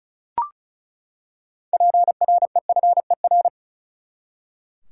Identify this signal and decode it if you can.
Morse code, heard as a keyed tone, about 36 wpm, 700 Hz. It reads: PREFER
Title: Six Letter Words - 35wpm